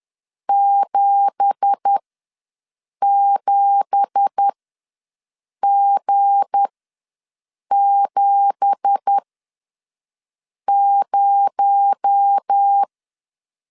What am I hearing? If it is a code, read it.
Morse code, heard as a keyed tone, about 10 wpm, spaced as for 5 wpm, 800 Hz. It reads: 77G70
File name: LEZIONI / 9 2867 ws 10wpm 4215